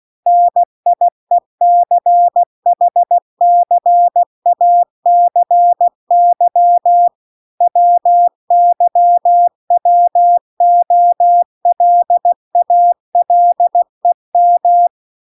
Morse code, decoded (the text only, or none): NIECHCACY WYWOLALEM